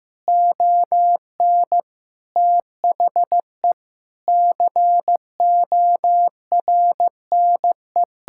Morse code, heard as a keyed tone, about 15 wpm, 700 Hz. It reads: ON THE CORNE
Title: Sentences from Top 1600 Words 15wpm